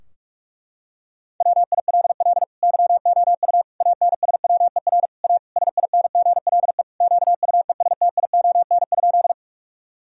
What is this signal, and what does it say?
Morse code, heard as a keyed tone, about 46 wpm, 700 Hz. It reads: WILL YOU ANSWER A SINGLE QUESTION?
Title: Sentences from Top 1000 Words - Review Part 4 - 45wpm